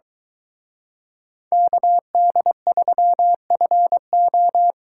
Morse code, heard as a keyed tone, about 23 wpm, 700 Hz. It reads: KD3FO